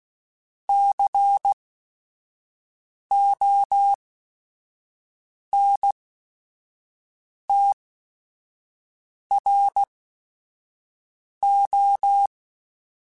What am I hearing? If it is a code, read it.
Morse code, heard as a keyed tone, about 16 wpm, spaced as for 5 wpm, 800 Hz. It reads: CONTRO
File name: Technical Subjects / Technical Subjects-mp3s / OR Gate @5